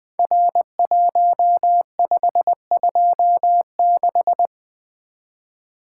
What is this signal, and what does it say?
Morse code, heard as a keyed tone, about 20 wpm, 700 Hz. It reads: R1526
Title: Code Groups - 5 Letter:Numbers - 20wpm